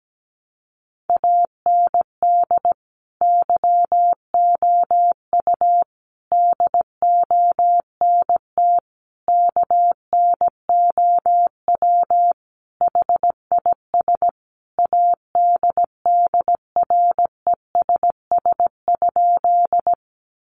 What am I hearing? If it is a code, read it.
Morse code, heard as a keyed tone, about 17 wpm, 700 Hz. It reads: AND YOU DONT KNOW HIS ADDRESS?
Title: Sentences from Top 1000 Words 17wpm